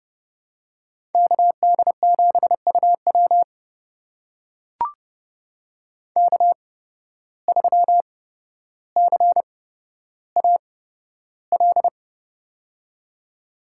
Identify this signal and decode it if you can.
Morse code, heard as a keyed tone, about 31 wpm, spaced as for 13 wpm, 700 Hz. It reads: KD7UW K3CAL
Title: USA Call Signs - 5 Characters - 8x Character Spacing - 30wpm